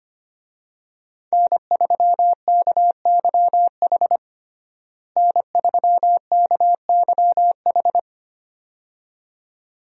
Morse code, heard as a keyed tone, about 25 wpm, 700 Hz. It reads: N3KYH N3KYH
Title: USA Call Signs - 5 Characters - Repeated 2x - 25wpm